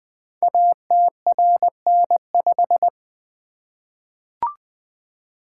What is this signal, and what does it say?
Morse code, heard as a keyed tone, about 20 wpm, 700 Hz. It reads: ATRN5
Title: Code Groups - 5 Letter:Numbers - 20wpm